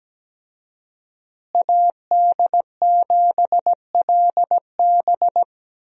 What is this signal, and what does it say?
Morse code, heard as a keyed tone, about 17 wpm, 700 Hz. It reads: AD7LB
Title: USA Call Signs - 5 Characters - Repeated 3x - 17wpm